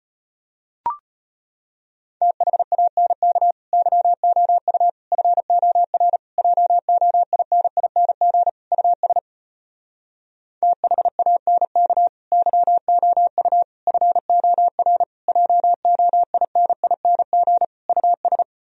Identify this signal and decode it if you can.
Morse code, heard as a keyed tone, about 33 wpm, 700 Hz. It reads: THANK YOU FOR JOINING US THANK YOU FOR JOINING US